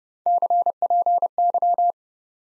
Morse code, heard as a keyed tone, about 30 wpm, 700 Hz. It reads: CPY
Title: Top 90 Abbr 30WPM